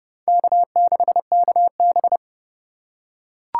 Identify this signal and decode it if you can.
Morse code, heard as a keyed tone, about 32 wpm, 700 Hz. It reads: K6KB
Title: USA Call Signs - 2x Character Spacing - 30wpm